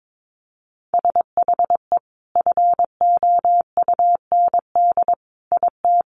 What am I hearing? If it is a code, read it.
Morse code, heard as a keyed tone, about 22 wpm, 700 Hz. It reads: SHE FOUND IT
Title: Sentences from Top 1100 Words 22wpm